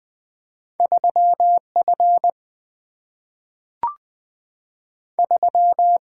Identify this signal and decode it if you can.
Morse code, heard as a keyed tone, about 20 wpm, 700 Hz. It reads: 3F 3
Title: Single-Letter Single-Number 20wpm